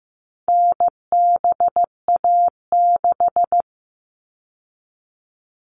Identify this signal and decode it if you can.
Morse code, heard as a keyed tone, about 15 wpm, 700 Hz. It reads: NBA6
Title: All Cut Numbers - 4-Digit 15wpm